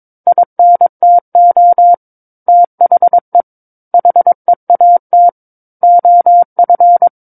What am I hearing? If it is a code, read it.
Morse code, heard as a keyed tone, about 22 wpm, 700 Hz. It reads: INTO THE HEAT OF
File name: Book One / 13-Chapter 13 - 22 WPM